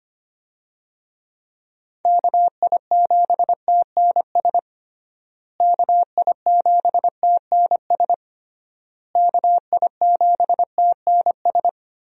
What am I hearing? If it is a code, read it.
Morse code, heard as a keyed tone, about 25 wpm, 700 Hz. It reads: KI7TNS KI7TNS KI7TNS
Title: USA Call Signs - 6 Characters - Repeated 3x - 25wpm